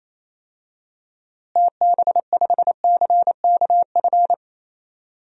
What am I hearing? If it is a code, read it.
Morse code, heard as a keyed tone, about 28 wpm, 700 Hz. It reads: TB5CKF
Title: DXCC-100 Call Signs - 6 Characters - Repeated 2x - 28wpm